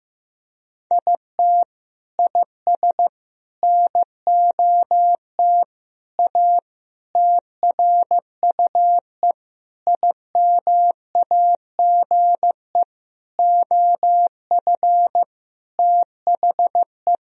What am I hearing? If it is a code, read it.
Morse code, heard as a keyed tone, about 15 wpm, 700 Hz. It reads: IT IS NOT A TRUE IMAGE OF THE
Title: Sentences from Top 1000 Words - Review Part 4 - 15wpm